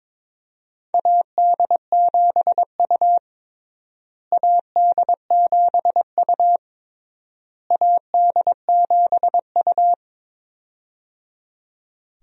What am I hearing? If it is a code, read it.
Morse code, heard as a keyed tone, about 22 wpm, 700 Hz. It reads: AD7U AD7U AD7U
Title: USA Call Signs - 4 Characters - Repeated 3x - 22wpm